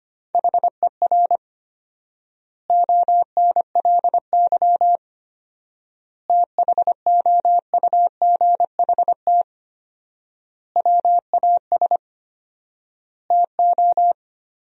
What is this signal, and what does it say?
Morse code, heard as a keyed tone, about 25 wpm, 700 Hz. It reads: HER ONLY THOUGHT WAS TO
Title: 4x Word-Spacing - Sentences from Top 400 Words - 25wpm